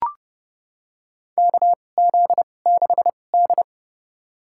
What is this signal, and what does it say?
Morse code, heard as a keyed tone, about 30 wpm, spaced as for 22 wpm, 690 Hz. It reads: KZ6D
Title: USA Call Signs - 4 Characters - 2x Character Spacing - 30wpm